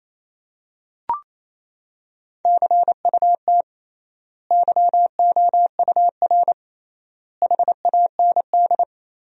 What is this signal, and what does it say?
Morse code, heard as a keyed tone, about 27 wpm, 700 Hz. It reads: CUT YOUR HAND
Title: 3x Word-Spacing - Sentences from Top 300 Words - 28wpm